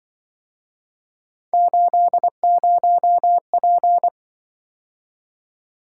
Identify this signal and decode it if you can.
Morse code, heard as a keyed tone, about 24 wpm, 700 Hz. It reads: 80P
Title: Speed-Racing - Three Letter-Number Combinations - 20wpm